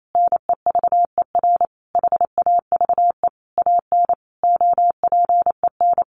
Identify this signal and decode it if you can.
Morse code, heard as a keyed tone, about 28 wpm, 700 Hz. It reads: NEVER HAVE AN OPEN
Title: Sentences from Top 1400 Words 28wpm